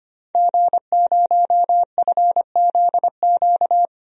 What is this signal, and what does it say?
Morse code, heard as a keyed tone, about 25 wpm, 700 Hz. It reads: G0FZQ
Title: DXCC-100 Call Signs - 5 Characters - 25wpm